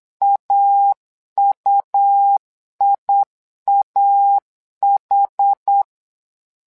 Morse code, heard as a keyed tone, about 8 wpm, 800 Hz. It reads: AUIAH
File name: LEZIONI / 2 UHDN / 2 UH 8wpm 2129